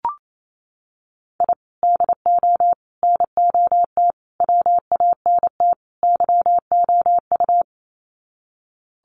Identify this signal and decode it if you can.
Morse code, heard as a keyed tone, about 28 wpm, 700 Hz. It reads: I DO NOT WANT YOU
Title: Sets of 5 Words 28wpm